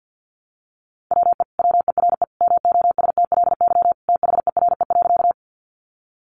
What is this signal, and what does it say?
Morse code, heard as a keyed tone, about 50 wpm, 700 Hz. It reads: WE WERE MOSTLY THERE.
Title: Sentences from Top 1300 Words 50wpm